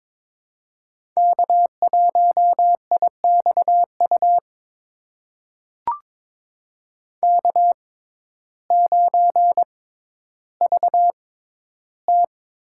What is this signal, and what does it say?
Morse code, heard as a keyed tone, about 22 wpm, spaced as for 12 wpm, 700 Hz. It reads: K1IXU K9VT
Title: USA Call Signs - 5 Characters - 6x Character Spacing - 22wpm